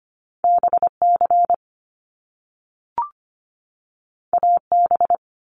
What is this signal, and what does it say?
Morse code, heard as a keyed tone, about 25 wpm, 700 Hz. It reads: BC AB
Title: Canadian Provinces and Territories 25wpm